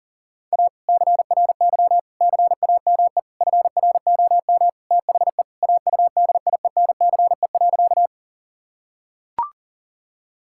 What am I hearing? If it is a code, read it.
Morse code, heard as a keyed tone, about 40 wpm, 700 Hz. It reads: A CRY CAME FROM THE AUDIENCE.
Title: Sentences from Top 1000 Words - Review Part 5 - 40wpm